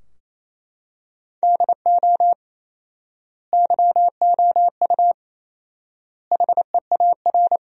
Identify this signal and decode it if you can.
Morse code, heard as a keyed tone, about 27 wpm, 700 Hz. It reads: DO YOU HEAR
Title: 4x Word-Spacing - Sentences from Top 300 Words - 28wpm